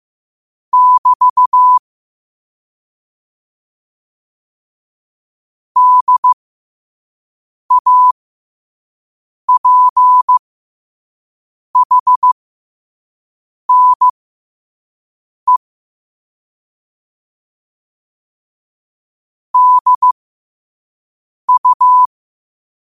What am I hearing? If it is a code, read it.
Morse code, heard as a keyed tone, about 15 wpm, spaced as for 5 wpm, 1000 Hz. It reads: <BT> DAPHNE DU
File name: qotdfiles / 5wpm